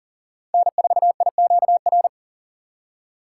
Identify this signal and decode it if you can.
Morse code, heard as a keyed tone, about 40 wpm, 700 Hz. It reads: N4IQR
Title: USA Call Signs - 5 Characters - 2x Character Spacing - 40wpm